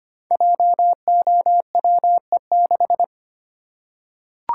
Morse code, heard as a keyed tone, about 25 wpm, 700 Hz. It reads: JOWE6
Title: Code Groups - 5 Letter:Numbers - 25wpm